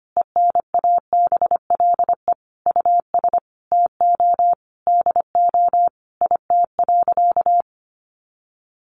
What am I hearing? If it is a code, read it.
Morse code, heard as a keyed tone, about 25 wpm, 700 Hz. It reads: ENABLE US TO DO IT.
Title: Sentences from Top 1500 Words 25wpm